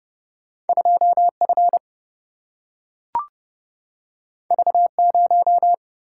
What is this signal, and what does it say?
Morse code, heard as a keyed tone, about 30 wpm, 700 Hz. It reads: 2F V0
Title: Single-Letter Single-Number 30wpm